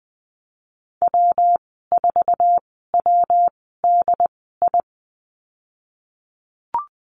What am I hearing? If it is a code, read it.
Morse code, heard as a keyed tone, about 20 wpm, spaced as for 15 wpm, 700 Hz. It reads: W4WDI
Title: 122 Lesson 20g - Number 3 - Call Signs 20wpm